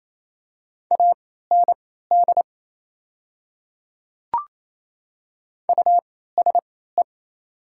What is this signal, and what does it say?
Morse code, heard as a keyed tone, about 28 wpm, spaced as for 16 wpm, 690 Hz. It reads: AND USE
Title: 3x Character Spacing - Top 100 Words - 28wpm